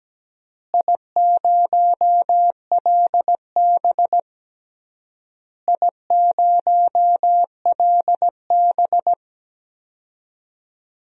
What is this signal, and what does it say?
Morse code, heard as a keyed tone, about 17 wpm, 700 Hz. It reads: I0LB I0LB
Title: DXCC-100 Call Signs - 4 Characters - Repeated 3x - 17wpm